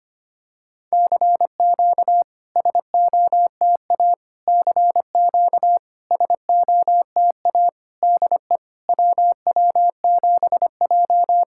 Morse code, heard as a keyed tone, about 25 wpm, 690 Hz. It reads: CQ SOTA CQ SOTA DE WW7J
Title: SOTA QSO Practice 25wpm